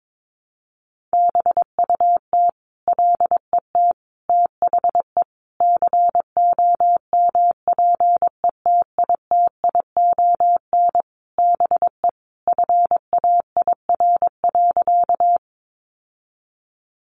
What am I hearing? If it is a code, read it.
Morse code, heard as a keyed tone, about 22 wpm, 700 Hz. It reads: BUT LET THE COMPETITION BE FAIR.
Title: Sentences from Top 1300 Words 22wpm